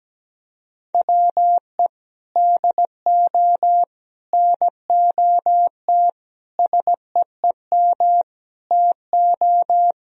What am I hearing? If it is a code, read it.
Morse code, heard as a keyed tone, about 17 wpm, 700 Hz. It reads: WE DO NOT SEEM TO
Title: Sentences from Top 1000 Words - Review Part 1 - 17wpm